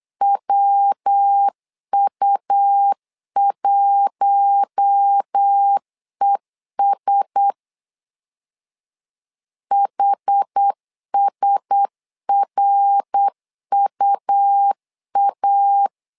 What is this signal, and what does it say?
Morse code, heard as a keyed tone, about 8 wpm, 800 Hz. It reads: WU1ES HSRUA
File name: LEZIONI / 4 Q1S 8wpm 2710